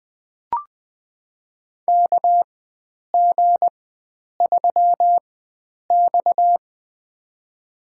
Morse code, heard as a keyed tone, about 20 wpm, spaced as for 9 wpm, 700 Hz. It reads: KG3X
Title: USA Call Signs - 4x Character Spacing - 20wpm